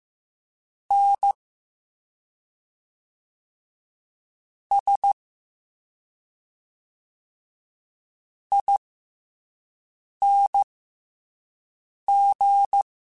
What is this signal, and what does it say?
Morse code, heard as a keyed tone, about 15 wpm, spaced as for 5 wpm, 800 Hz. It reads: N S ING